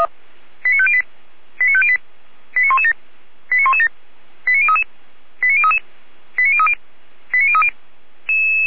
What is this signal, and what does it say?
Five-tone selective call (ZVEI), heard as quick runs of five tones, tones of about 70 ms. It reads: 89508 89508 89117 89117 89033 89033 89030 89030